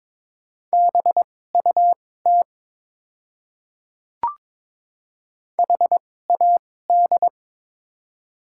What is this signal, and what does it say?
Morse code, heard as a keyed tone, about 22 wpm, spaced as for 16 wpm, 700 Hz. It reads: BUT HAD
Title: 2x Character Spacing - Top 100 Words - 22wpm